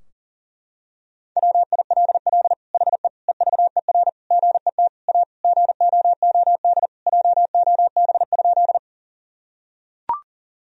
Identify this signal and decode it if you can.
Morse code, heard as a keyed tone, about 40 wpm, 700 Hz. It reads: WILL HE EVER GET A GOOD JOB?
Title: Sentences from Top 1000 Words - Review Part 3 - 40wpm